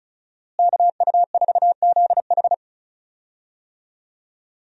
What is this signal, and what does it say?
Morse code, heard as a keyed tone, about 36 wpm, 700 Hz. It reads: KU4ZH
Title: USA Call Signs - 5 Characters - Repeated 3x - 35wpm